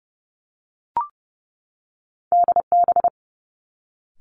Morse code, heard as a keyed tone, about 30 wpm, 700 Hz. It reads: DB
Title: Two Challenging Letter-Number Combinations 30wpm